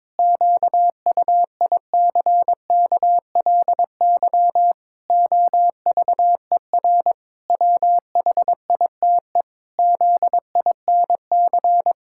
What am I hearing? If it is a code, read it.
Morse code, heard as a keyed tone, about 22 wpm, 700 Hz. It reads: QUICKLY OVER WHITE ZINC
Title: Pangram Challenge v2 - 22wpm